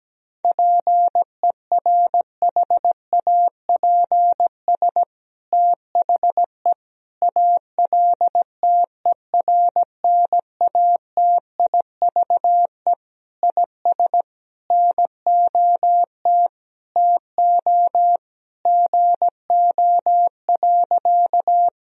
Morse code, heard as a keyed tone, about 17 wpm, 700 Hz. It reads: PERHAPS THE ALTERNATIVE IS NOT TO GO.
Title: Sentences from Top 1000 Words - Review Part 4 - 17wpm